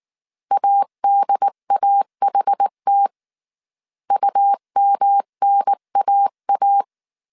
Morse code, heard as a keyed tone, about 19 wpm, 800 Hz. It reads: ADAHT UMNAA